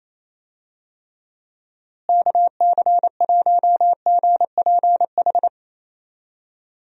Morse code, heard as a keyed tone, about 28 wpm, 700 Hz. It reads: KC1GPH
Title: USA Call Signs - 6 Characters - Repeated 3x - 28wpm